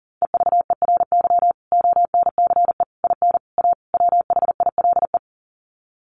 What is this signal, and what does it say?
Morse code, heard as a keyed tone, about 40 wpm, 700 Hz. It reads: EVERY ONCE IN A WHILE
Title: Sets of 5 Words 40wpm